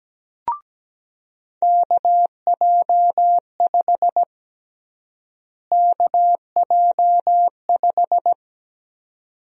Morse code, heard as a keyed tone, about 17 wpm, 700 Hz. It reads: KJ5 KJ5
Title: USA Call Sign Prefixes - Repeated 2x - 17wpm